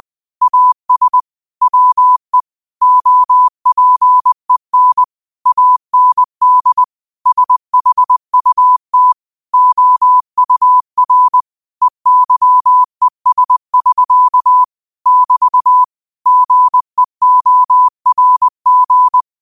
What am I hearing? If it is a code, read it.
Morse code, heard as a keyed tone, about 20 wpm, 1000 Hz. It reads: AS WE OPEN AND SHUT OUR EYES<SK> <BT> GEORG
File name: qotdfiles / 20wpm